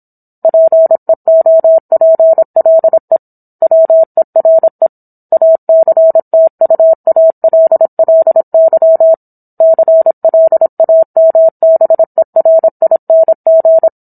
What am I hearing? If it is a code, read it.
Morse code, heard as a keyed tone, about 27 wpm, 650 Hz. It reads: PEOPLE WERE ACTUALLY CLAMBERING
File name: Book One / 17-Chapter 17 - 26 WPM